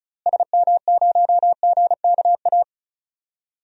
Morse code, heard as a keyed tone, about 36 wpm, 700 Hz. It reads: SM0GKA
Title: DXCC-100 Call Signs - 6 Characters - Repeated 3x - 35wpm